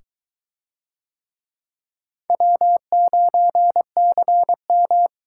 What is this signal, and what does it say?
Morse code, heard as a keyed tone, about 21 wpm, 700 Hz. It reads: W9CM